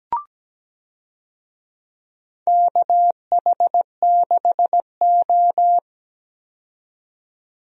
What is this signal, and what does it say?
Morse code, heard as a keyed tone, about 17 wpm, 700 Hz. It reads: KH6O
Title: DXCC-100 Call Signs - 4 Characters - 17wpm